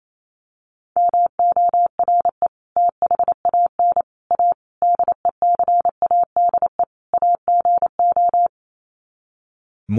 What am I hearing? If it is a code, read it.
Morse code, heard as a keyed tone, about 28 wpm, 700 Hz. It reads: MORE THAN A DECADE AGO
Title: Sets of 5 Words Encore2 28wpm